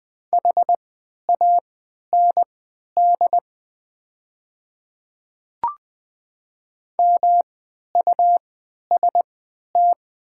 Morse code, heard as a keyed tone, about 20 wpm, spaced as for 11 wpm, 700 Hz. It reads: HAND MUST
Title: 3x Character Spacing - Top 200 Words - 20wpm